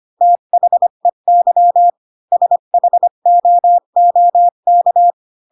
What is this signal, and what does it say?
Morse code, heard as a keyed tone, about 25 wpm, 700 Hz. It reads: THEY SHOOK